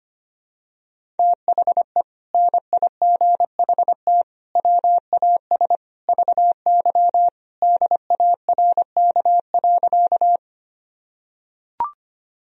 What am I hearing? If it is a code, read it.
Morse code, heard as a keyed tone, about 25 wpm, 700 Hz. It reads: THE NIGHT WAS VY DARK.
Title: Sentences from Top 500 Words and Abbreviations 25wpm